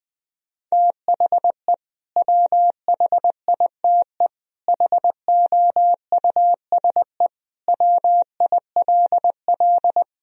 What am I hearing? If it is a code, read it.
Morse code, heard as a keyed tone, about 20 wpm, 700 Hz. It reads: THE WHITE HOUSE WILL
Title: Sentences from Top 1000 Words - Review Part 1 - 20wpm